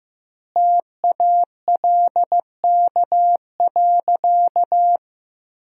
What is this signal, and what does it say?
Morse code, heard as a keyed tone, about 15 wpm, 700 Hz. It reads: TALK.
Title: 2x Word-Spacing - Sentences from Top 400 Words - 15wpm